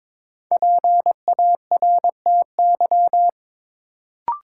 Morse code, heard as a keyed tone, about 23 wpm, 700 Hz. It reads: PARTY
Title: Sentences from Top 1400 Words - New Word Drill Alt - 22wpm